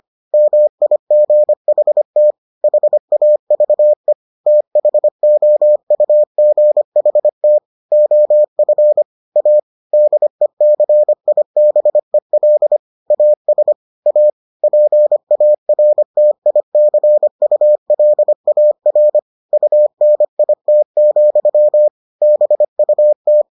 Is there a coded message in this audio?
Yes: Morse code from 0.3 to 23.4 s, heard as a keyed tone, about 25 wpm, 600 Hz. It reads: MIGHT HAVE THOUGHT OF A DECIBEL AS A PARTICULAR UNIT, BUT